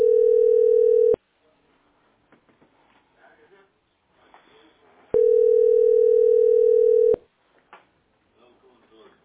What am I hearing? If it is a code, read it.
Ringback tone (North America), 440 + 480 Hz, with speech.
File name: ddcbacd3-9cca-4acc-8fe8